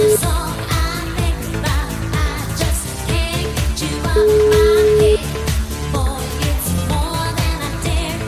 Ringback tone (Europe, 425 Hz), with noise.